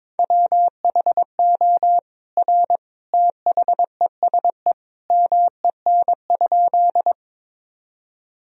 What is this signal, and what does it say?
Morse code, heard as a keyed tone, about 22 wpm, 700 Hz. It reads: WHO R THESE MEN?
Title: Sentences from Top 200 Words and Abbreviations 22wpm